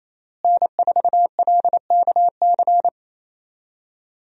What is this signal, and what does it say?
Morse code, heard as a keyed tone, about 28 wpm, 700 Hz. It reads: N4LKC
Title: USA Call Signs - 5 Characters - Repeated 2x - 28wpm